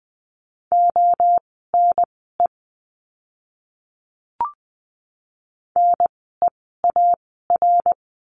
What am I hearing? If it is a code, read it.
Morse code, heard as a keyed tone, about 20 wpm, spaced as for 15 wpm, 700 Hz. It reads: ONE NEAR
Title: 061 Lesson 11f - Letter h - Words 20wpm